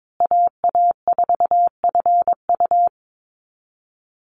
Morse code, heard as a keyed tone, about 22 wpm, 700 Hz. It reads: AA4FU
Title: Call Signs Repeated 2x - 22wpm